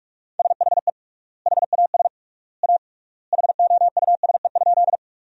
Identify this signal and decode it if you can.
Morse code, heard as a keyed tone, about 46 wpm, 700 Hz. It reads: SHE HAS A HOUSE?
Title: 3x Word-Spacing - Sentences from Top 200 Words - 45wpm